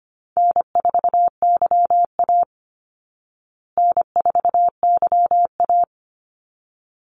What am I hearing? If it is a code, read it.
Morse code, heard as a keyed tone, about 25 wpm, 700 Hz. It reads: N4YA N4YA
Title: Call Signs Repeated 3x - 25wpm